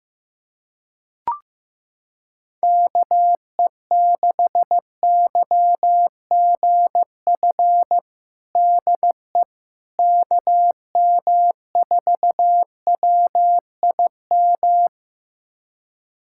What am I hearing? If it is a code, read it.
Morse code, heard as a keyed tone, about 15 wpm, 700 Hz. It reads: KE6YGF DE KM4WIM
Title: USA Callsign DE Callsign - 6 Characters - 15wpm